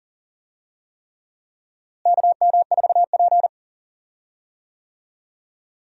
Morse code, heard as a keyed tone, about 37 wpm, 700 Hz. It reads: KM4P
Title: USA Call Signs - 4 Characters - Repeated 3x - 40wpm